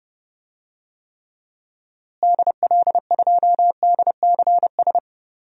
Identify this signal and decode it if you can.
Morse code, heard as a keyed tone, about 32 wpm, 700 Hz. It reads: DL2DCS
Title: DXCC-100 Call Signs - 6 Characters - Repeated 3x - 30wpm